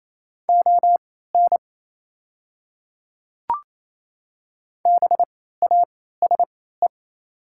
Morse code, heard as a keyed tone, about 28 wpm, spaced as for 16 wpm, 700 Hz. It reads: ON BASE
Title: 3x Character Spacing - Top 500 Words - 28wpm